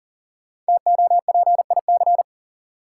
Morse code, heard as a keyed tone, about 39 wpm, 700 Hz. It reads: TOPIC
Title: Sentences from Top 1200 Words - New Word Drill Alt - 40wpm